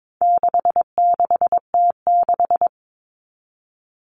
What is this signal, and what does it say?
Morse code, heard as a keyed tone, about 22 wpm, 700 Hz. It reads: 66T6
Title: Common Cut Numbers - 4-Digit 22wpm